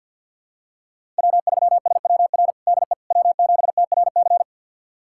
Morse code, heard as a keyed tone, about 49 wpm, 700 Hz. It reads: W3SWR DE W7TRC
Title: USA Callsign DE Callsign - 5 Characters - 50wpm